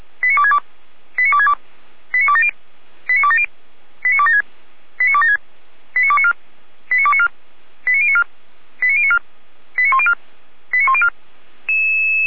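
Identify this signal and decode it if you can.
Five-tone selective call (ZVEI), heard as quick runs of five tones, tones of about 70 ms. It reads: 89262 89262 89270 89270 89276 89276 89294 89294 89094 89094 89104 89104
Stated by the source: Five-tone ZVEI selective call received by SDR, archived by the fire brigade (Feuerwehr) Orlamünde.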